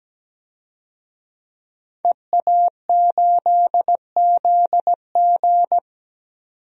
Morse code, heard as a keyed tone, about 17 wpm, 700 Hz. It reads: EA8ZG
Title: DXCC-100 Call Signs - 5 Characters - 17wpm